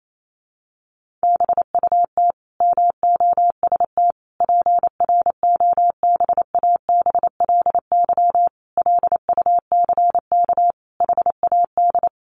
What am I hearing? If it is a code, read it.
Morse code, heard as a keyed tone, about 28 wpm, 700 Hz. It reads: BUT MOST PROBABLY LUCK HAD
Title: Sentences from Top 1500 Words 28wpm